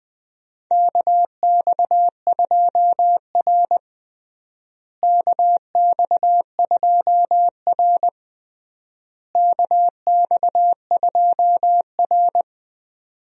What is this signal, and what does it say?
Morse code, heard as a keyed tone, about 20 wpm, 700 Hz. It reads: KX2R KX2R KX2R
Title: USA Call Signs - 4 Characters - Repeated 3x - 20wpm